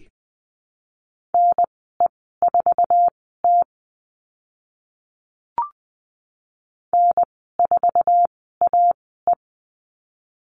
Morse code, heard as a keyed tone, about 20 wpm, spaced as for 14 wpm, 700 Hz. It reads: NE4T N4AE
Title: 062 Lesson 11g - Letter h - Call Signs 20wpm